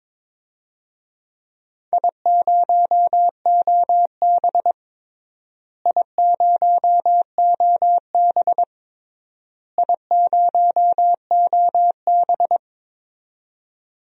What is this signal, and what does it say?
Morse code, heard as a keyed tone, about 22 wpm, 700 Hz. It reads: I0OB I0OB I0OB
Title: DXCC-100 Call Signs - 4 Characters - Repeated 3x - 22wpm